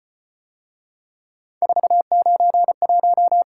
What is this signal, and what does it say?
Morse code, heard as a keyed tone, about 31 wpm, 700 Hz. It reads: 491
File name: Speed-Racing - Three Letter-Number Combinations - 25wpm